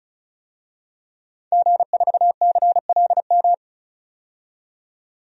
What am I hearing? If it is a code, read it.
Morse code, heard as a keyed tone, about 36 wpm, 700 Hz. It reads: G4CLM
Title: DXCC-100 Call Signs - 5 Characters - 35wpm